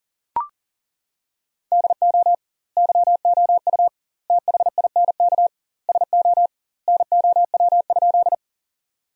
Morse code, heard as a keyed tone, about 40 wpm, 700 Hz. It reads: DO YOU THINK SO NOW?
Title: 2x Word-Spacing - Sentences from Top 200 Words - 40wpm